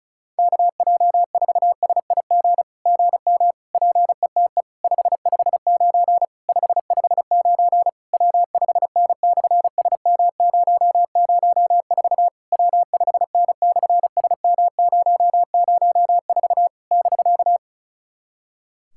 Morse code, heard as a keyed tone, about 34 wpm, 700 Hz. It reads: KJ4SIG GM PETE 559 559 W5N/SM004 W5N/SM004 <BK>
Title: SOTA QSO Practice 35wpm